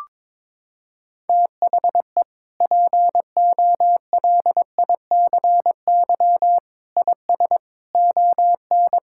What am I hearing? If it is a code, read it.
Morse code, heard as a keyed tone, about 22 wpm, 700 Hz. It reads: THE POLICY IS ON
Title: Sentences from Top 1000 Words - Review Part 5 - 22wpm